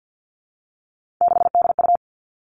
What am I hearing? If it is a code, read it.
Morse code, heard as a keyed tone, about 50 wpm, 700 Hz. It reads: 6DU
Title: Three Challenging Letter-Number Combinations 50wpm